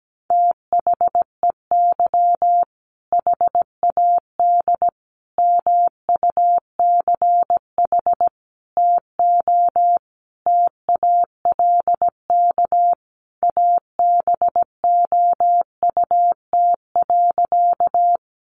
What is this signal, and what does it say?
Morse code, heard as a keyed tone, about 17 wpm, 700 Hz. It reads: THEY HAD MUCH TO TALK ABOUT.
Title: Sentences from Top 400 Words 17wpm